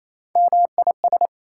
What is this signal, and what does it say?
Morse code, heard as a keyed tone, about 28 wpm, 700 Hz. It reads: MIS
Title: Prefixes and Suffixes 28wpm